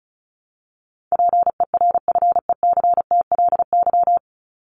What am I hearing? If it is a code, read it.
Morse code, heard as a keyed tone, about 34 wpm, 700 Hz. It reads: PERFECTLY
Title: Top 1501-1750 Words 35wpm